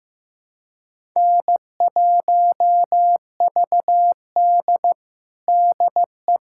Morse code, heard as a keyed tone, about 15 wpm, 700 Hz. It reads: N1VD DE
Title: USA Callsign DE Callsign - 4 Characters - 15wpm